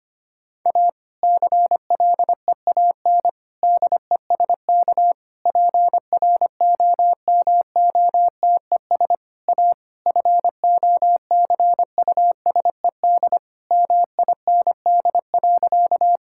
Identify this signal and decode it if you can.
Morse code, heard as a keyed tone, about 25 wpm, 700 Hz. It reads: A CLEAN DESK PROMOTES A FOCUSED MIND.
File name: practice / 25wpm / Difficult Sentences with Hints - 25wpm